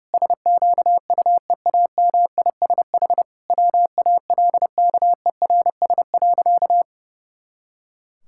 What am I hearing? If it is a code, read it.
Morse code, heard as a keyed tone, about 30 wpm, 700 Hz. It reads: SQUEAMISH WALKERS.
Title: Pangram Challenge v2 - 30wpm